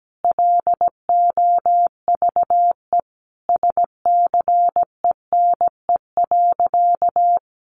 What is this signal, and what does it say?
Morse code, heard as a keyed tone, about 17 wpm, 700 Hz. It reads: LOVE SCENE.
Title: Sentences from Top 1300 Words 17wpm